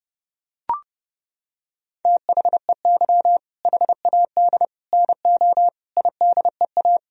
Morse code, heard as a keyed tone, about 30 wpm, 700 Hz. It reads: THEY HAD NO IDEA
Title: Sentences from Top 1000 Words - Review Part 1 - 30wpm